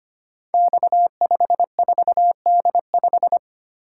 Morse code, heard as a keyed tone, about 25 wpm, 700 Hz. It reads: X54D5
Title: Code Groups - 5 Letter:Numbers - 25wpm